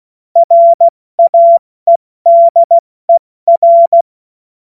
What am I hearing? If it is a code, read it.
Morse code, heard as a keyed tone, about 16 wpm, 680 Hz. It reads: RAEDER
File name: Morsekurs / Witze / Witz 21. Juni 2022 70 F80